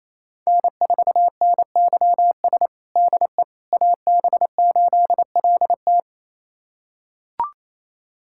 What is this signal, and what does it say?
Morse code, heard as a keyed tone, about 28 wpm, 700 Hz. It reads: N4NYS DE AB8LT
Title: USA Callsign DE Callsign - 5 Characters - 28wpm